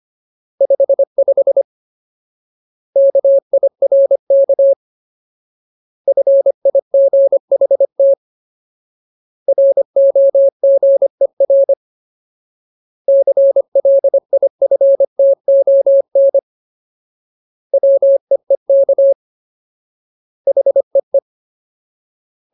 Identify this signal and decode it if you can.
Morse code, heard as a keyed tone, about 25 wpm, 550 Hz. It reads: 55 KIRK FIGHT ROGER CLIFTON WEEK HEE
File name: Ultimate Word List / 25WPM Ultimate